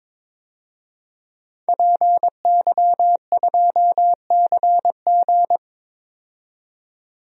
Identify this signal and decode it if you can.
Morse code, heard as a keyed tone, about 22 wpm, 700 Hz. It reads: PY2CG
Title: DXCC-100 Call Signs - 5 Characters - 22wpm